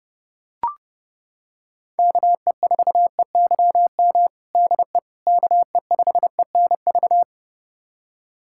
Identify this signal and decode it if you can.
Morse code, heard as a keyed tone, about 32 wpm, 700 Hz. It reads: KE4EYM DE KE5ENV
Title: USA Callsign DE Callsign - 6 Characters - 30wpm